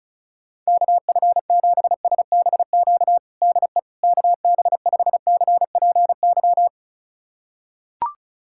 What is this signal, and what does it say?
Morse code, heard as a keyed tone, about 34 wpm, 700 Hz. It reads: KF7SBQ DE KB5CPY
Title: USA Callsign DE Callsign 35wpm